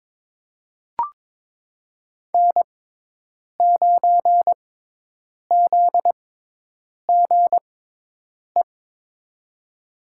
Morse code, heard as a keyed tone, about 22 wpm, spaced as for 8 wpm, 700 Hz. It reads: N9ZGE
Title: USA Call Signs - 6x Character Spacing - 22wpm